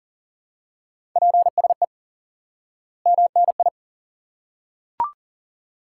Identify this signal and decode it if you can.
Morse code, heard as a keyed tone, about 40 wpm, 700 Hz. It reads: PSE MNI
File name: IQE Course 3 - Lesson 89 - YAGI - Two Elements - 40wpm